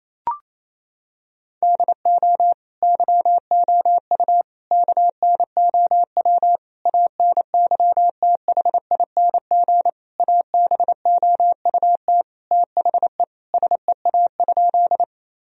Morse code, heard as a keyed tone, about 27 wpm, 700 Hz. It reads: DO YOU KNOW ANYTHING ABOUT THE SEA?
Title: Sentences from Top 1000 Words - Review Part 1 - 28wpm